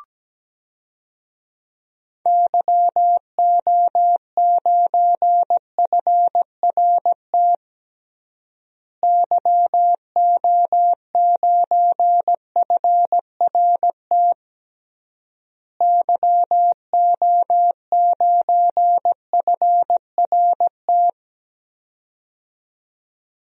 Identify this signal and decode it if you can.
Morse code, heard as a keyed tone, about 17 wpm, 700 Hz. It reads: YO9FRT YO9FRT YO9FRT
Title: DXCC-100 Call Signs - 6 Characters - Repeated 3x - 17wpm